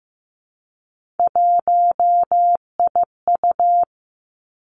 Morse code, heard as a keyed tone, about 15 wpm, 700 Hz. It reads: 1IU
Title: Three Challenging Letter-Number Combinations 15wpm